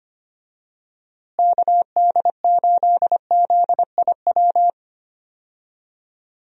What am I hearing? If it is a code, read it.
Morse code, heard as a keyed tone, about 25 wpm, 700 Hz. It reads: KD8ZIW
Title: USA Call Signs - 6 Characters - Repeated 2x - 25wpm